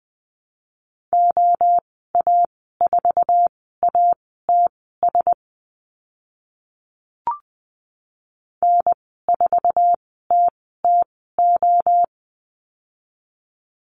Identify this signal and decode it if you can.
Morse code, heard as a keyed tone, about 20 wpm, spaced as for 14 wpm, 700 Hz. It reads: OA4ATS N4TTO
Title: 062 Lesson 11g - Letter h - Call Signs 20wpm